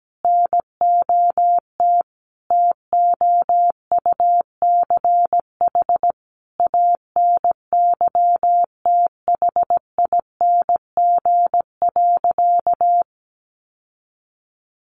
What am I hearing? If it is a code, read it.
Morse code, heard as a keyed tone, about 17 wpm, 700 Hz. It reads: NOT TOUCH ANYTHING.
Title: Sentences from Top 1200 Words 17wpm